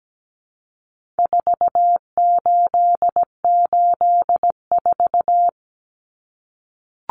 Morse code, heard as a keyed tone, about 17 wpm, 700 Hz. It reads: 4884
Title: Common Cut Numbers - 4-Digit 17wpm